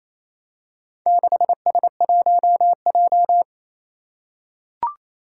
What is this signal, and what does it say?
Morse code, heard as a keyed tone, about 27 wpm, 700 Hz. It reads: 6S1J
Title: Four Challenging Letter-Number Combinations 28wpm